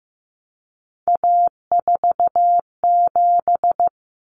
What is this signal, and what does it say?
Morse code, heard as a keyed tone, about 15 wpm, 700 Hz. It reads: A47
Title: Common Cut Numbers - 3-Digit 15wpm